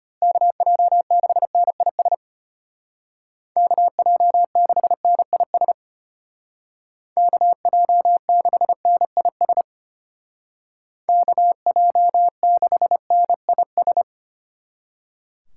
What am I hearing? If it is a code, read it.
Morse code, heard as a keyed tone, about 31 wpm, 700 Hz. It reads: KJ6NIS KJ6NIS KJ6NIS KJ6NIS
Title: Speed Racing - USA 2x3 Call Signs 25wpm